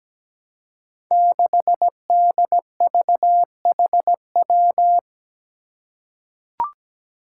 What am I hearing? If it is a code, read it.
Morse code, heard as a keyed tone, about 17 wpm, 700 Hz. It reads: 6DVHW
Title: Five Challenging Letter-Number Combinations 17wpm